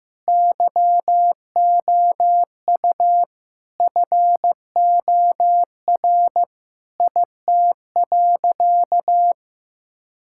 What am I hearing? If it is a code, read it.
Morse code, heard as a keyed tone, about 16 wpm, 700 Hz. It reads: YOU FOR IT.
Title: Sentences from Top 1000 Words - Review Part 6 - 15wpm